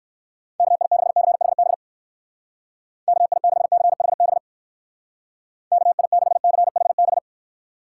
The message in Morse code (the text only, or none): KI6CHB KI6CHB KI6CHB